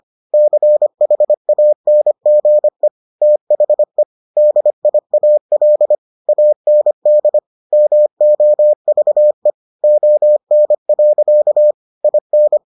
Morse code, heard as a keyed tone, about 25 wpm, 600 Hz. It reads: CHANGE THE DIAL AND MOVE ON. IN